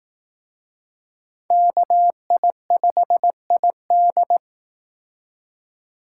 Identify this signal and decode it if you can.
Morse code, heard as a keyed tone, about 18 wpm, 700 Hz. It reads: KI5ID
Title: Speed Racing - USA 2x2 Call Signs 15wpm